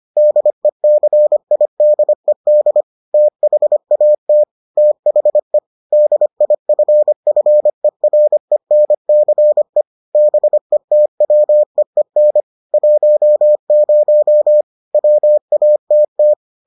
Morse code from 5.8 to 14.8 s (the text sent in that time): DIFFERENCE BETWEEN 10